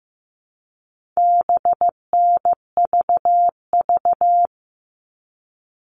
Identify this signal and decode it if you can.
Morse code, heard as a keyed tone, about 15 wpm, 700 Hz. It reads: BNVV
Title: All Cut Numbers - 4-Digit 15wpm